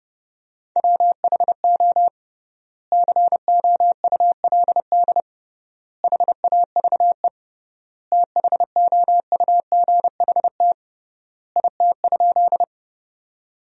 Morse code, heard as a keyed tone, about 32 wpm, 700 Hz. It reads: WHO COULD HAVE THOUGHT IT?
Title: 3x Word-Spacing - Sentences from Top 300 Words - 30wpm